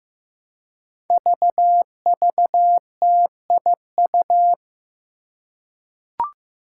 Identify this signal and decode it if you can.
Morse code, heard as a keyed tone, about 15 wpm, 700 Hz. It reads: VVTIU
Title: Five Challenging Letter-Number Combinations 15wpm